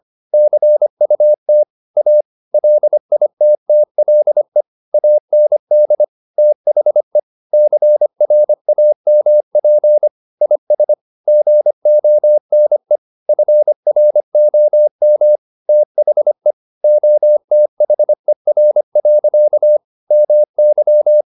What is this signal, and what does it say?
Morse code, heard as a keyed tone, about 25 wpm, 600 Hz. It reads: CUT A LITTLE AND THE CRAMP IS GONE FROM THE OTHER. MY